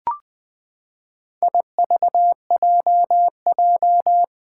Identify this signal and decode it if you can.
Morse code, heard as a keyed tone, about 20 wpm, 700 Hz. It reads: IVJJ
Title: Four Challenging Letter-Number Combinations 20wpm